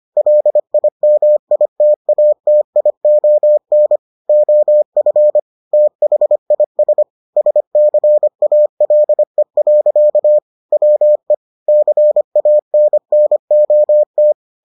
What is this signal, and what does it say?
Morse code, heard as a keyed tone, about 25 wpm, 600 Hz. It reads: LIMITATION OF THIS SCALE. WE CANNOT